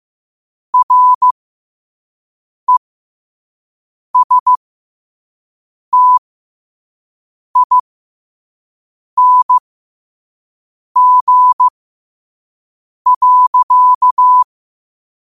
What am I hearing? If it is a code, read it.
Morse code, heard as a keyed tone, about 15 wpm, spaced as for 5 wpm, 1000 Hz. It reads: RESTING.
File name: qotdfiles / 5wpm